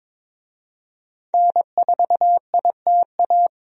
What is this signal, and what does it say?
Morse code, heard as a keyed tone, about 22 wpm, 700 Hz. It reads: N4ITA
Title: USA Call Signs - 5 Characters - 22wpm